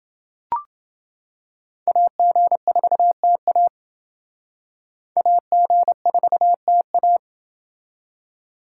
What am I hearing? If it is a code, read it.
Morse code, heard as a keyed tone, about 28 wpm, 700 Hz. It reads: AG4TA AG4TA